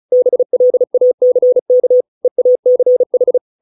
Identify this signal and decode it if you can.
Morse code, heard as a keyed tone, about 34 wpm, 500 Hz. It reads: BLACK EACH